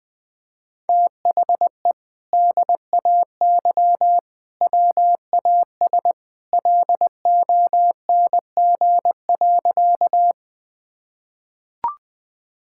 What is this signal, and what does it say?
Morse code, heard as a keyed tone, about 20 wpm, 700 Hz. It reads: THE DAY WAS LONG.
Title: Sentences with 2-Word Phrases - 20wpm